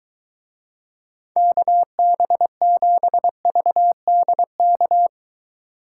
Morse code, heard as a keyed tone, about 23 wpm, 700 Hz. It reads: KB7VDK